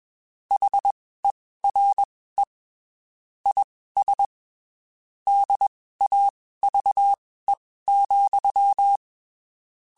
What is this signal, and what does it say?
Morse code, heard as a keyed tone, about 21 wpm, spaced as for 15 wpm, 800 Hz. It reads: HERE IS DAVE,
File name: (spaced) / QSOs Spaced-mp3s / WD0BGP 1 @20 sp